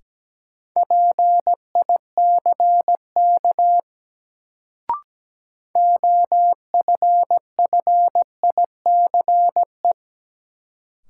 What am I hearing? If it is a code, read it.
Morse code, heard as a keyed tone, about 17 wpm, 700 Hz. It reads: PICK OFFICE
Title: Sentences from Top 1000 Words - New Word Drill - 17wpm